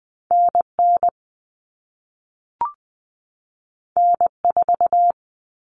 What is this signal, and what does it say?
Morse code, heard as a keyed tone, about 20 wpm, 700 Hz. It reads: NN N4
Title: All Cut Numbers - 2-Digit 20wpm